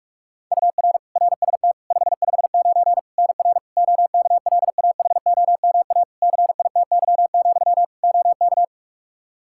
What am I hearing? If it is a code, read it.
Morse code, heard as a keyed tone, about 41 wpm, 700 Hz. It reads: UR RST 559 NR OKLAHOMA CITY, OK
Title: Practice QSO - Exchange 1 - Beginning - 45wpm